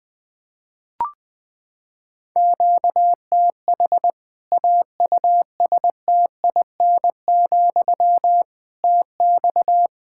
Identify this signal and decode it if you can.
Morse code, heard as a keyed tone, about 20 wpm, 700 Hz. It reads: QTH AUSTIN, TX
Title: Practice QSO - Exchange 1 - Beginning - 20wpm